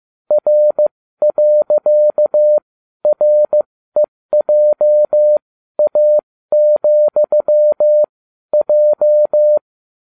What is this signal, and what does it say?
Morse code, heard as a keyed tone, about 15 wpm, spaced as for 12 wpm, 600 Hz. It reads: R.REJA,J